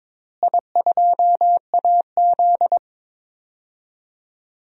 Morse code, heard as a keyed tone, about 22 wpm, 700 Hz. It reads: I2AZ
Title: DXCC-100 Call Signs - 4 Characters - Repeated 2x - 22wpm